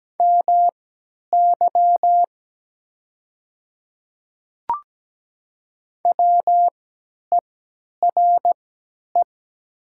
Morse code, heard as a keyed tone, about 17 wpm, spaced as for 10 wpm, 700 Hz. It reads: MY WERE
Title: 3x Character Spacing - Top 100 Words - 17wpm